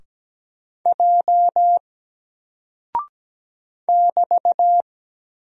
Morse code, heard as a keyed tone, about 17 wpm, 700 Hz. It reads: J <BT>
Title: Single Character (Letter:Number:Punctuation:Prosign) Alt 17wpm